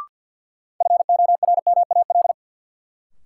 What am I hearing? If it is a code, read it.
Morse code, heard as a keyed tone, about 53 wpm, 700 Hz. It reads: FORMAL
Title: Sentences from Top 1300 Words - New Word Drill Alt - 50wpm